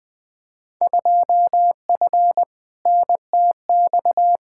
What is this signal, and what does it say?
Morse code, heard as a keyed tone, about 20 wpm, 700 Hz. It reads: 2F NTX
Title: Field Day Exchanges - 20wpm